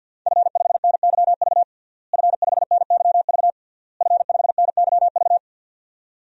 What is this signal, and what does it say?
Morse code, heard as a keyed tone, about 52 wpm, 700 Hz. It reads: F5NYV F5NYV F5NYV
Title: DXCC-100 Call Signs - 5 Characters - Repeated 3x - 50wpm